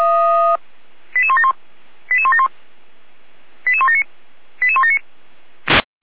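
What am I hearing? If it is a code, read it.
Five-tone selective call (ZVEI), heard as quick runs of five tones, tones of about 70 ms. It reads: 88171 88171 88179 88179
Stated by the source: ZVEI five-tone selective call received by SDR.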